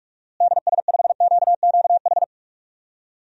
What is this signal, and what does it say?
Morse code, heard as a keyed tone, about 42 wpm, 700 Hz. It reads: DS5QQH
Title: DXCC-100 Call Signs - 6 Characters - Repeated 3x - 45wpm